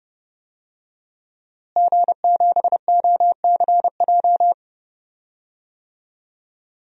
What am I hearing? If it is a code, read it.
Morse code, heard as a keyed tone, about 30 wpm, 700 Hz. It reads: G7OCJ
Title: DXCC-100 Call Signs - 5 Characters - Repeated 3x - 30wpm